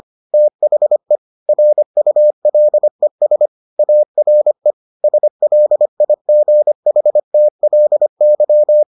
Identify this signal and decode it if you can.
Morse code, heard as a keyed tone, about 25 wpm, 600 Hz. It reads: THE RULES ARE SLIGHTLY